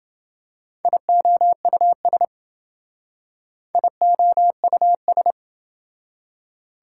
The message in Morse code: IOUS IOUS

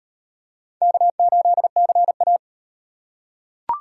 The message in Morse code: K8CA

